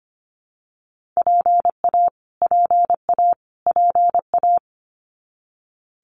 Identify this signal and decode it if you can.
Morse code, heard as a keyed tone, about 25 wpm, 700 Hz. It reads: PA PA PA
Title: US States Repeated 3x 25wpm